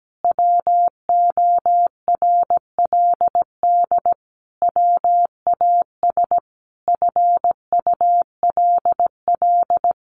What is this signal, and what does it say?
Morse code, heard as a keyed tone, about 17 wpm, 700 Hz. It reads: WORLD WAS FULL
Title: Sentences from Top 1300 Words 17wpm